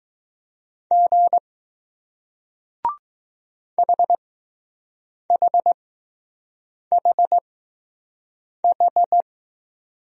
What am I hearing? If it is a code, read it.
Morse code, heard as a keyed tone, about 20 wpm, spaced as for 6 wpm, 700 Hz. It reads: G HHHH